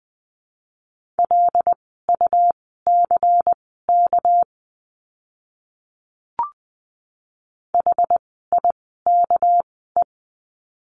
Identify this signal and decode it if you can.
Morse code, heard as a keyed tone, about 19 wpm, spaced as for 14 wpm, 700 Hz. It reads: LUCK HIKE
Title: 199 Lesson 32f - Letter k - Words 20wpm